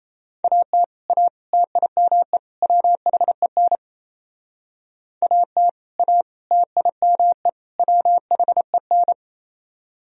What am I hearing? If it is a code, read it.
Morse code, heard as a keyed tone, about 34 wpm, 700 Hz. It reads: AT A TIME WHEN AT A TIME WHEN
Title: Speed-Racing - Sets of 4 Words 28wpm